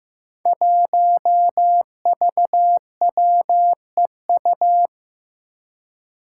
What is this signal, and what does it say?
Morse code, heard as a keyed tone, about 15 wpm, 700 Hz. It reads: 1VWEU
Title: Five Challenging Letter-Number Combinations 15wpm